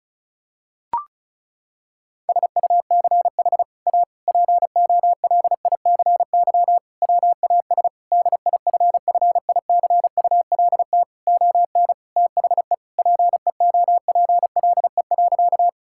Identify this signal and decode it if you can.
Morse code, heard as a keyed tone, about 34 wpm, 700 Hz. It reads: SUCH A POLICY WAS DIFFICULT ON THE PEOPLE.
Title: Sentences from Top 1000 Words - Review Part 5 - 35wpm